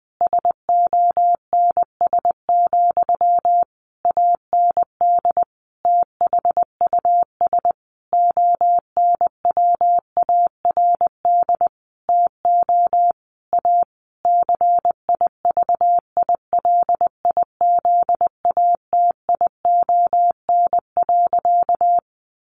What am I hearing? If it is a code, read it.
Morse code, heard as a keyed tone, about 20 wpm, 700 Hz. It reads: SONS, AND THUS ONWARD TO A CIVILIZATION.
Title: A Sound Of Thunder 20wpm